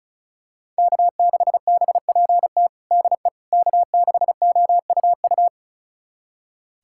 Morse code, heard as a keyed tone, about 36 wpm, 700 Hz. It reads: K6BPT DE K6OUU
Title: USA Callsign DE Callsign - 5 Characters - 35wpm